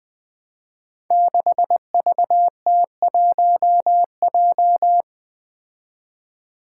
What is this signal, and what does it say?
Morse code, heard as a keyed tone, about 20 wpm, 700 Hz. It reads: 6VT1J
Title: Five Challenging Letter-Number Combinations 20wpm